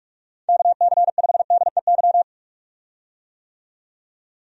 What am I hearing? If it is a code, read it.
Morse code, heard as a keyed tone, about 46 wpm, 700 Hz. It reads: KC5DEY
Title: USA Call Signs - 6 Characters - 45wpm